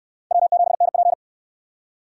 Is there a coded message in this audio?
Yes: Morse code from 0.3 to 1.1 s, heard as a keyed tone, about 60 wpm, 700 Hz. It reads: W7NP